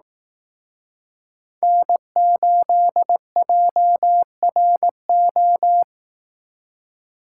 Morse code, heard as a keyed tone, about 18 wpm, 700 Hz. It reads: N8JRO
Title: Speed Racing - USA 1x3 Call Signs 15wpm